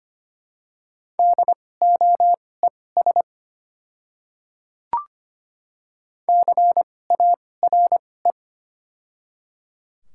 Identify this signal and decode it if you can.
Morse code, heard as a keyed tone, about 25 wpm, spaced as for 18 wpm, 700 Hz. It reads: DOES CARE
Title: 2x Character Spacing - Top 300 Words - 25wpm